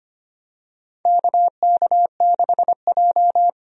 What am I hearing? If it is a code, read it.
Morse code, heard as a keyed tone, about 25 wpm, 700 Hz. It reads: KK6J
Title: USA Call Signs - 4 Characters - 25wpm